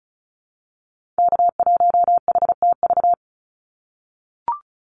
Morse code, heard as a keyed tone, about 34 wpm, 700 Hz. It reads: K1HTV
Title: Call Signs - 35wpm